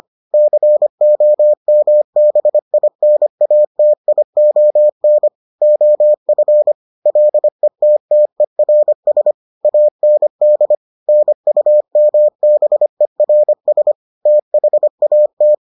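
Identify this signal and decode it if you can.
Morse code, heard as a keyed tone, about 25 wpm, 600 Hz. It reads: COMBINATION OF LETTERS AND NUMBERS THAT
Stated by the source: Morse edition of the Foundations of Amateur Radio episodes.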